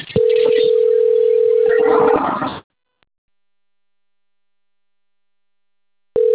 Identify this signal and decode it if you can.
Ringback tone (North America), 440 + 480 Hz.
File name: rg-1600-unknown- - - .94746